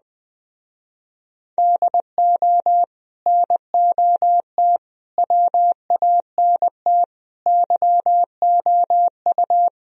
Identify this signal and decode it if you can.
Morse code, heard as a keyed tone, about 20 wpm, 700 Hz. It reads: DO NOT WANT YOU
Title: Speed-Racing - Sets of 5 Words 17wpm